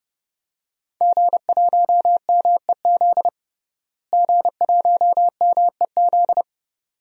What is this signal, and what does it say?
Morse code, heard as a keyed tone, about 30 wpm, 700 Hz. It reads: G1MEZ G1MEZ
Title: DXCC-100 Call Signs - 5 Characters - Repeated 2x - 30wpm